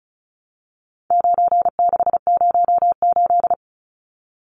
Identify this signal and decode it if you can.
Morse code, heard as a keyed tone, about 34 wpm, 700 Hz. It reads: 9608
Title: Numbers - 4-Digit 35wpm